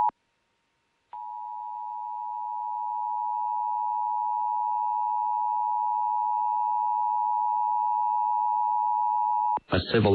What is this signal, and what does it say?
Emergency Alert System (SAME) tones (the 853 and 960 Hz attention tone), with speech. It is heard from 1.1 to 9.6 s.